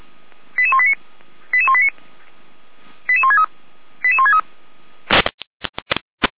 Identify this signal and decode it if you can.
Five-tone selective call (ZVEI), heard as quick runs of five tones, tones of about 70 ms. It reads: 88189 88189 88163 88163